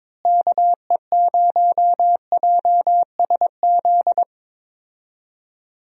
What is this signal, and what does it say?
Morse code, heard as a keyed tone, about 22 wpm, 700 Hz. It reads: KE0JSZ
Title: USA Call Signs - 6 Characters - Repeated 2x - 22wpm